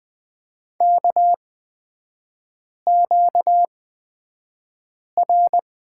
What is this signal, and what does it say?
Morse code, heard as a keyed tone, about 20 wpm, spaced as for 5 wpm, 700 Hz. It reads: KQR
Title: 243 Lesson 39e - Letter z - Three Characters 20wpm